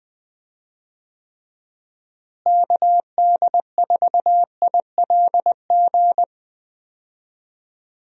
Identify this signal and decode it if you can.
Morse code, heard as a keyed tone, about 21 wpm, 700 Hz. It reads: KD4ILG
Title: USA Call Signs - 6 Characters - Repeated 2x - 20wpm